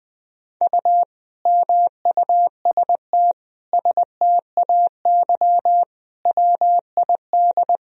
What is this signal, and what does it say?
Morse code, heard as a keyed tone, about 21 wpm, 700 Hz. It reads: U MUST STAY WID
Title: Sentences from Top 500 Words and Abbreviations 20wpm